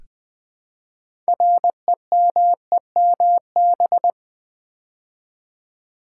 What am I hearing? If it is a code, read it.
Morse code, heard as a keyed tone, about 20 wpm, 700 Hz. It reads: REMEMB
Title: Build A Word - 3 Letters At A Time - 8-Letter Words - 20wpm